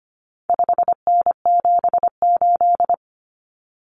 Morse code, heard as a keyed tone, about 25 wpm, 700 Hz. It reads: 5N78
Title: Common Cut Numbers - 4-Digit 25wpm